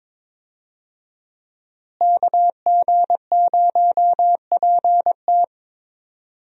Morse code, heard as a keyed tone, about 21 wpm, 700 Hz. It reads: KG0PT
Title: USA Call Signs - 5 Characters - Repeated 3x - 22wpm